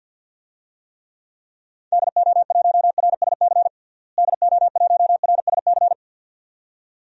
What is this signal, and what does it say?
Morse code, heard as a keyed tone, about 51 wpm, 700 Hz. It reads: DO1RSC DO1RSC
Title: DXCC-100 Call Signs - 6 Characters - Repeated 2x - 50wpm